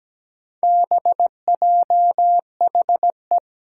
Morse code, heard as a keyed tone, about 17 wpm, 700 Hz. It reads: BJHE
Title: Four Challenging Letter-Number Combinations 17wpm